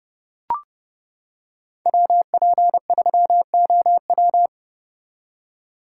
Morse code, heard as a keyed tone, about 29 wpm, 700 Hz. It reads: WP3OW